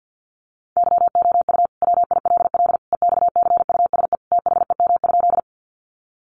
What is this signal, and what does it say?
Morse code, heard as a keyed tone, about 50 wpm, 700 Hz. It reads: YOU WILL EXCUSE THEM?
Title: Sentences from Top 700 Words 50wpm